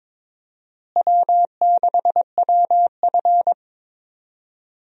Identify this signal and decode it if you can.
Morse code, heard as a keyed tone, about 22 wpm, 700 Hz. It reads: W6WF
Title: USA Call Signs - 4 Characters - Repeated 2x - 22wpm